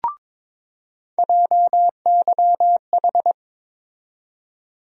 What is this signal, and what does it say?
Morse code, heard as a keyed tone, about 22 wpm, 700 Hz. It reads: JYH
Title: USA Call Sign Postfixes - 3 Letters 22wpm